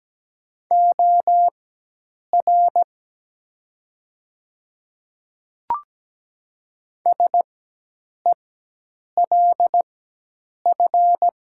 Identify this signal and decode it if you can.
Morse code, heard as a keyed tone, about 17 wpm, spaced as for 8 wpm, 700 Hz. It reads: OR SELF
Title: 4x Character Spacing - Top 500 Words - 17wpm